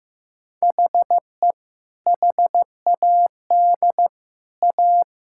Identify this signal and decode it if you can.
Morse code, heard as a keyed tone, about 15 wpm, 700 Hz. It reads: HE HAD A
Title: Sentences from Top 1000 Words - Review Part 5 - 15wpm